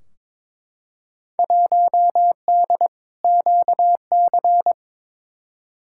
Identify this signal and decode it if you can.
Morse code, heard as a keyed tone, about 22 wpm, 700 Hz. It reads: 1D QC
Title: Field Day Exchanges - 22wpm